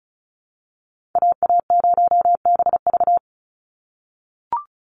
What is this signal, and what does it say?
Morse code, heard as a keyed tone, about 34 wpm, 700 Hz. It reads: AA0BV
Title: NA SOTA Activator Callsigns 35wpm